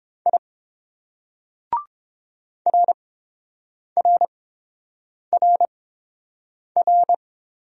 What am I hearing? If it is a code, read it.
Morse code, heard as a keyed tone, about 30 wpm, spaced as for 7 wpm, 690 Hz. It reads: I RRRR